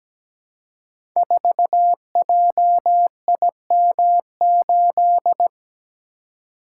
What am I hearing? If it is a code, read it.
Morse code, heard as a keyed tone, about 17 wpm, 700 Hz. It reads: 4JIM8
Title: Code Groups - 5 Letter:Numbers - 17wpm